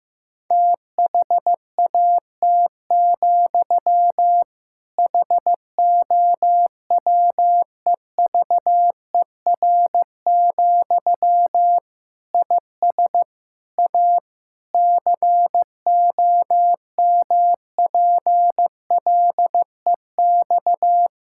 Morse code, heard as a keyed tone, about 15 wpm, 700 Hz. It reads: THAT, HOWEVER, IS A COMPLEX
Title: Sentences from Top 1000 Words - Review Part 6 - 15wpm